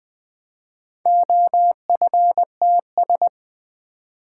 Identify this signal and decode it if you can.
Morse code, heard as a keyed tone, about 20 wpm, 700 Hz. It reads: OFTS
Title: Four Letter-Number Combinations - 20wpm